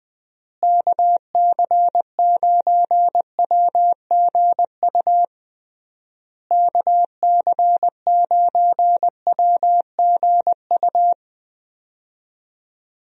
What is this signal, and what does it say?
Morse code, heard as a keyed tone, about 20 wpm, 700 Hz. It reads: KC9WGU KC9WGU
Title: USA Call Signs - 6 Characters - Repeated 3x - 20wpm